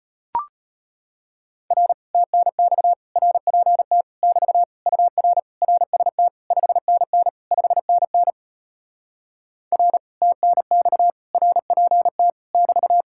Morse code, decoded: R TNX RPT <BT> UR RST 5NN 5NN R TNX RPT <BT>

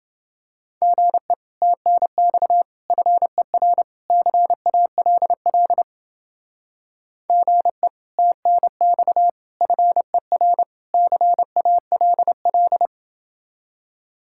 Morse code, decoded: GE TNX FER CALL GE TNX FER CALL

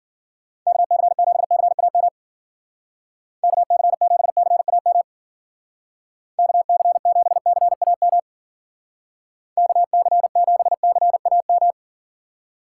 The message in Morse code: KC7CAM KC7CAM KC7CAM KC7CAM

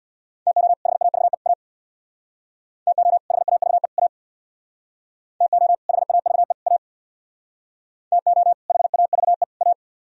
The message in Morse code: TO HAVE A TO HAVE A TO HAVE A TO HAVE A

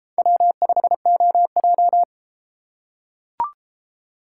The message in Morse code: W5OJ

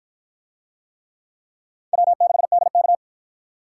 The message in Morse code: W6DK

